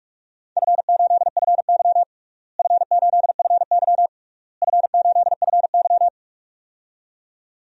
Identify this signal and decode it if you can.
Morse code, heard as a keyed tone, about 46 wpm, 700 Hz. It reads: F8FY F8FY F8FY
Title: DXCC-100 Call Signs - 4 Characters - Repeated 3x - 45wpm